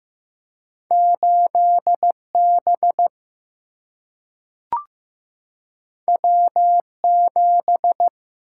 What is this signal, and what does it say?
Morse code, heard as a keyed tone, about 15 wpm, 700 Hz. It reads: 8 B W 7
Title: Single-Letter Single-Number 15wpm